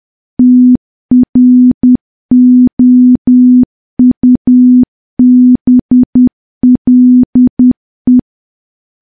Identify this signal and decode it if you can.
Morse code, heard as a keyed tone, about 10 wpm, 250 Hz. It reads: TROUBLE
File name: morse / morse-training-10WPM